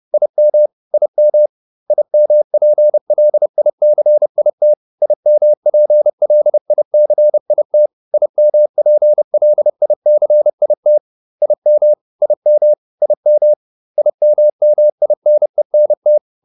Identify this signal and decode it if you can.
Morse code, heard as a keyed tone, about 30 wpm, 600 Hz. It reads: IM IM IMPLICIT IMPLICIT IMPLICIT IM IM IM IMMINENT
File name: IM8 30